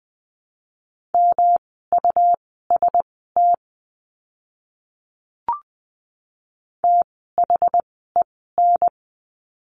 Morse code, heard as a keyed tone, about 20 wpm, spaced as for 14 wpm, 700 Hz. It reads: MUST THEN
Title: 108 Lesson 18f - Letter m - Words 20wpm